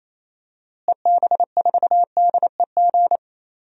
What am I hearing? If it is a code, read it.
Morse code, heard as a keyed tone, about 28 wpm, 700 Hz. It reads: EB4DEG
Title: DXCC-100 Call Signs - 6 Characters - Repeated 2x - 28wpm